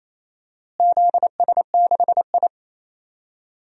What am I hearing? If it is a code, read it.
Morse code, heard as a keyed tone, about 28 wpm, 700 Hz. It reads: ZS6I
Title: DXCC-100 Call Signs - 4 Characters - Repeated 3x - 28wpm